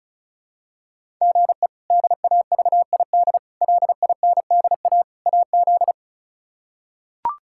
Morse code, heard as a keyed tone, about 36 wpm, 700 Hz. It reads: GE DAVID LINDA AZ
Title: K1USN SST QSO Practice - 35wpm